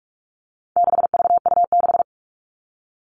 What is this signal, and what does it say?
Morse code, heard as a keyed tone, about 44 wpm, 700 Hz. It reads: 6VU6
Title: All Cut Numbers - 4-Digit 45wpm